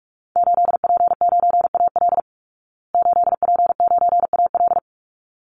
Morse code, heard as a keyed tone, about 44 wpm, 700 Hz. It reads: 8P9AL 8P9AL
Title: Call Signs Repeated 3x - 45wpm